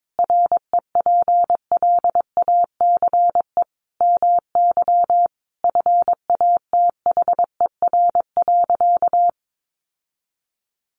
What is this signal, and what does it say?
Morse code, heard as a keyed tone, about 22 wpm, 700 Hz. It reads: REPLACE MY FATHER.
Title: Sentences from Top 1400 Words 22wpm